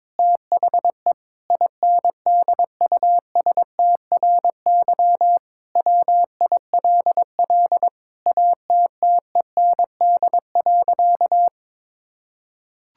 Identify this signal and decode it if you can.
Morse code, heard as a keyed tone, about 22 wpm, 700 Hz. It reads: THE INDUSTRY WILL ATTEND.
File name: Sentences with 7-Word Phrases - 22wpm